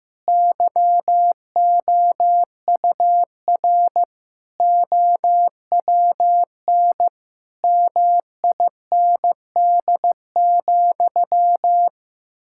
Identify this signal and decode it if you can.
Morse code, heard as a keyed tone, about 16 wpm, 700 Hz. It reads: YOUR OWN MIND,
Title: Sentences from Top 1000 Words - Review Part 6 - 15wpm